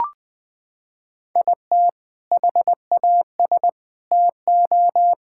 Morse code, heard as a keyed tone, about 20 wpm, 700 Hz. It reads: IT HAS TO
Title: Sentences with 4-Word Phrases - 20wpm